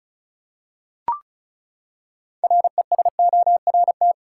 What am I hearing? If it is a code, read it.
Morse code, heard as a keyed tone, about 36 wpm, 700 Hz. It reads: RESORT
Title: Six Letter Words - 35wpm